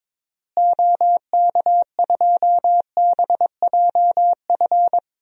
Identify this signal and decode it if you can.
Morse code, heard as a keyed tone, about 22 wpm, 700 Hz. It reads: OK2BJF
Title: DXCC-100 Call Signs - 6 Characters - Repeated 3x - 22wpm